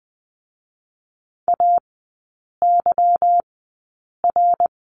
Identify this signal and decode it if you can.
Morse code, heard as a keyed tone, about 20 wpm, spaced as for 8 wpm, 700 Hz. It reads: AYR
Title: 152 Lesson 25e - Letter p - Three Characters 20wpm